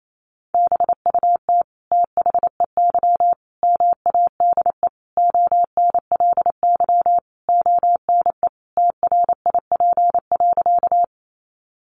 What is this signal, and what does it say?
Morse code, heard as a keyed tone, about 28 wpm, 700 Hz. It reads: BUT THEY MADE ONLY ONE TRIP.
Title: Sentences from Top 1300 Words 28wpm